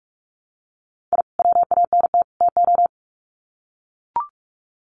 Morse code, heard as a keyed tone, about 44 wpm, 700 Hz. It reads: I WANT TO
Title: Sets of 3 Words 45wpm